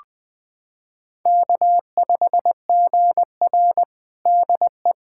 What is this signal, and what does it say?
Morse code, heard as a keyed tone, about 20 wpm, 700 Hz. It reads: K5GR DE
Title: USA Callsign DE Callsign 20wpm